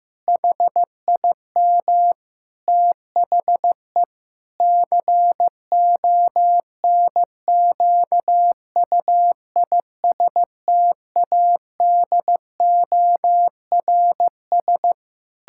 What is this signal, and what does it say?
Morse code, heard as a keyed tone, about 15 wpm, 700 Hz. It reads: HIM THE CONQUISTADORS
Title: Pangram Challenge v2 - 15wpm